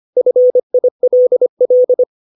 Morse code, heard as a keyed tone, about 25 wpm, 500 Hz. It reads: FILL
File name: 632 words / 632 Words 25WPM